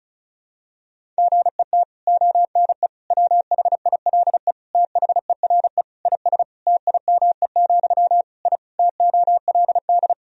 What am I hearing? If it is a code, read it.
Morse code, heard as a keyed tone, about 36 wpm, 700 Hz. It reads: GET ONE WHILE THERE IS TIME, I TOLD
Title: Sentences from Top 1000 Words - Review Part 1 - 35wpm